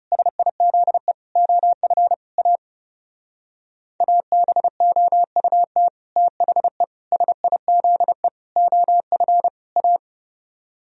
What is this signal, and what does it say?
Morse code, heard as a keyed tone, about 39 wpm, 700 Hz. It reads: SIZE OF A ABOUT THE SIZE OF A